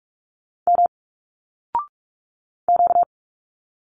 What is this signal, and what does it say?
Morse code, heard as a keyed tone, about 46 wpm, 700 Hz. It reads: M Q
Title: Single Letters 45wpm